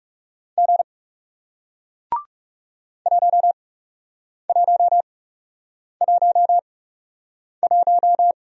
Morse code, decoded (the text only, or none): G 1111